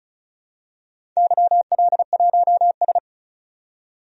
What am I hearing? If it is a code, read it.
Morse code, heard as a keyed tone, about 36 wpm, 700 Hz. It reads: YL1S
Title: DXCC-100 Call Signs - 4 Characters - 35wpm